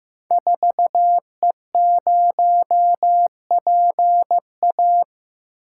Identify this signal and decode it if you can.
Morse code, heard as a keyed tone, about 15 wpm, 700 Hz. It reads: 4E0PA
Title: Code Groups - 5 Letter:Numbers - 15wpm